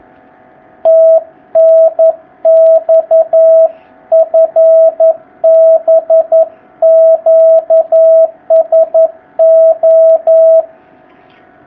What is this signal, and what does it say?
Morse code, heard as a keyed tone, about 11 wpm, 660 Hz. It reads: TNX FB QSO
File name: jh8cbh / 2019cw